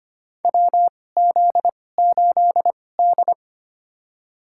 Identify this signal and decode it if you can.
Morse code, heard as a keyed tone, about 25 wpm, spaced as for 18 wpm, 700 Hz. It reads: WZ8D
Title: USA Call Signs - 4 Characters - 2x Character Spacing - 25wpm